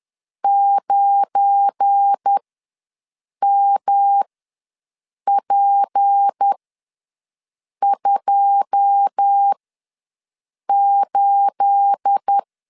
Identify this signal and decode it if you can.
Morse code, heard as a keyed tone, about 10 wpm, spaced as for 6 wpm, 800 Hz. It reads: 9MP28
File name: LEZIONI / 9 28 ws 10wpm 4124